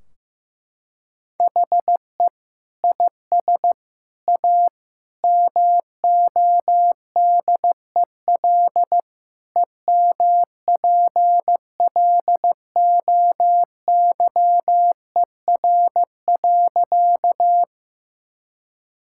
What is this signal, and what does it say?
Morse code, heard as a keyed tone, about 15 wpm, 700 Hz. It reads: HE IS A MODEL EMPLOYER.
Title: Sentences from Top 1600 Words 15wpm